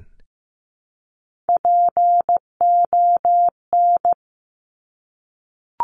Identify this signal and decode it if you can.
Morse code, heard as a keyed tone, about 15 wpm, 700 Hz. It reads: PON
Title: Top Three Letter Combinations in Words 15wpm